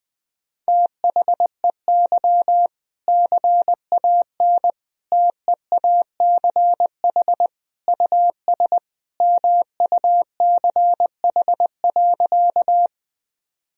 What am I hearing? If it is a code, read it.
Morse code, heard as a keyed tone, about 20 wpm, 700 Hz. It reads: THEY CAN TEACH US MUCH.
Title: Sentences from Top 1000 Words - Review Part 2 - 20wpm